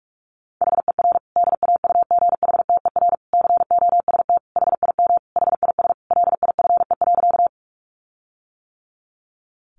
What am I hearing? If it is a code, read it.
Morse code, heard as a keyed tone, about 44 wpm, 700 Hz. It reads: HER DAUGHTER COST HIM HIS LIFE.
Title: Sentences from Top 1500 Words 45wpm